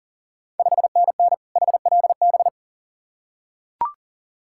Morse code, heard as a keyed tone, about 40 wpm, 700 Hz. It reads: 5NN HLB
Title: Ontario QSO Party Exchange 40wpm